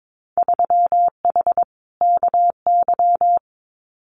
Morse code, heard as a keyed tone, about 22 wpm, 700 Hz. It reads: 3H KY
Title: Winter Field Day Exchanges Encore 22wpm